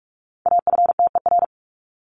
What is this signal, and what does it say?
Morse code, heard as a keyed tone, about 44 wpm, 700 Hz. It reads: AFTER
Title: Top 200 Words 45wpm